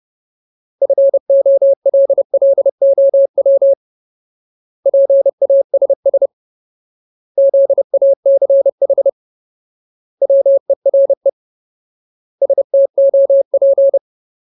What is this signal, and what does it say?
Morse code, heard as a keyed tone, about 30 wpm, 550 Hz. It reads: FOLLOW PASS ZACH WERE STOP
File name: Ultimate Word List / 30WPM Ultimate